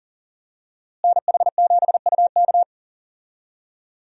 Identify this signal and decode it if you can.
Morse code, heard as a keyed tone, about 40 wpm, 700 Hz. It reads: NH7UK
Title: USA Call Signs - 5 Characters - Repeated 2x - 40wpm